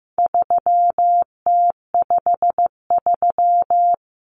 Morse code, heard as a keyed tone, about 15 wpm, 700 Hz. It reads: 3T53
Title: Common Cut Numbers - 4-Digit 15wpm